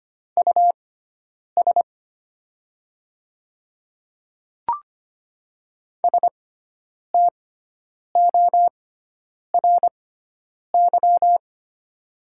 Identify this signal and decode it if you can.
Morse code, heard as a keyed tone, about 25 wpm, spaced as for 9 wpm, 700 Hz. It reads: US STORY
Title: 6x Character Spacing - Top 300 Words - 25wpm